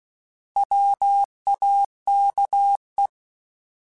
Morse code, heard as a keyed tone, about 16 wpm, 800 Hz. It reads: WAKE
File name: Alphabetical / Alphabetical mp3s / Ws @15